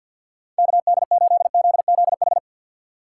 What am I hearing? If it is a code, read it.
Morse code, heard as a keyed tone, about 52 wpm, 700 Hz. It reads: KD8ZZH
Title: USA Call Signs - 6 Characters - 3x Character Spacing - 50wpm